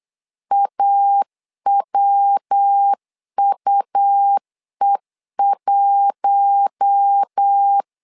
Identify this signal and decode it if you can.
Morse code, heard as a keyed tone, about 8 wpm, 800 Hz. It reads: AWUE1
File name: LEZIONI / 4 Q1 8wpm 3035